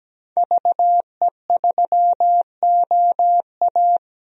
Morse code, heard as a keyed tone, about 17 wpm, 700 Hz. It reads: VE3OA
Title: DXCC-100 Call Signs - 5 Characters - 17wpm